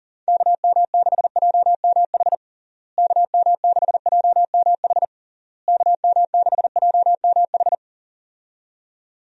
Morse code, heard as a keyed tone, about 40 wpm, 700 Hz. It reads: KM6JMH KM6JMH KM6JMH
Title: USA Call Signs - 6 Characters - Repeated 3x - 40wpm